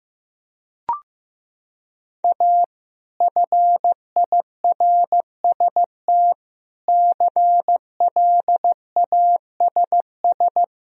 Morse code, heard as a keyed tone, about 15 wpm, 700 Hz. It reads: A FIRST CLASS
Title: Sentences from Top 1000 Words - Review Part 2 - 15wpm